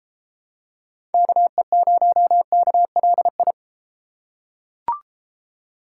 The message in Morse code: KE0KLI

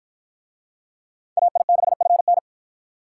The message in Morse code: AI6FN

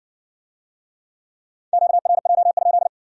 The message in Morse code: QRJ?

